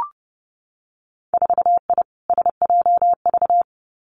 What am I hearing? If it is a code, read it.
Morse code, heard as a keyed tone, about 31 wpm, 700 Hz. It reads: 4I SJV
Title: Winter Field Day Exchanges Encore 30wpm